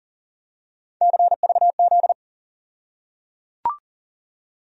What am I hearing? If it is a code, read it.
Morse code, heard as a keyed tone, about 37 wpm, 700 Hz. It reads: CVZ
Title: USA Call Sign Postfixes - 3 Letters 40wpm